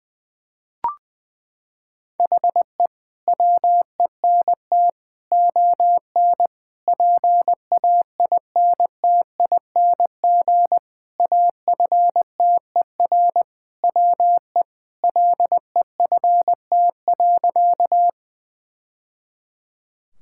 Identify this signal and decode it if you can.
Morse code, heard as a keyed tone, about 21 wpm, 700 Hz. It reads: HE WENT ON PAINTING AFTER WE LEFT.
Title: Sentences from Top 1000 Words - Review Part 6 - 20wpm